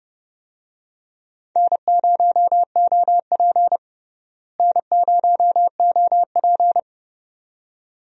Morse code, heard as a keyed tone, about 30 wpm, 700 Hz. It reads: N0OP N0OP
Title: USA Call Signs - 4 Characters - Repeated 2x - 30wpm